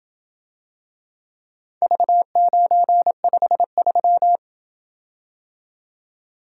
Morse code, heard as a keyed tone, about 27 wpm, 700 Hz. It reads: V953